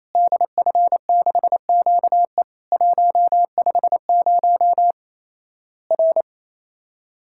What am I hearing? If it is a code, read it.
Morse code, heard as a keyed tone, about 28 wpm, 700 Hz. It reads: DF6QE 150 R
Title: Worked All Europe DX Contest - QTC Exchange - 28wpm